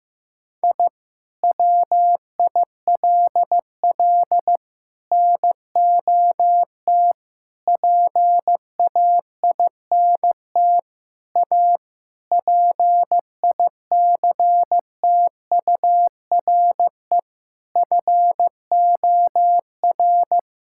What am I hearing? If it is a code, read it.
Morse code, heard as a keyed tone, about 15 wpm, 700 Hz. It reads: I WILL NOT PAINT A PICTURE FOR
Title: Sentences from Top 1000 Words - Review Part 6 - 15wpm